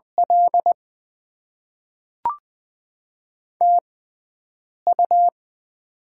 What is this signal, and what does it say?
Morse code, heard as a keyed tone, about 20 wpm, spaced as for 7 wpm, 700 Hz. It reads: L TU
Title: Top 20 QSO Elements - 6x Character Spacing - 20wpm